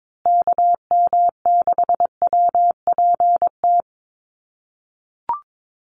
Morse code, heard as a keyed tone, about 22 wpm, 700 Hz. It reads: KM6WPT
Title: USA Call Signs Repeated 3x 22wpm